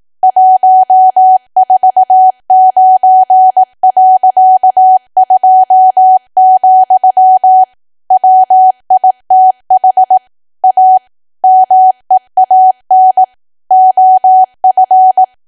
Morse code, heard as a keyed tone, about 18 wpm, 750 Hz. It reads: 149.2, WITH A MEAN OF
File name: cwa / W1AW-Level2 / 18WPM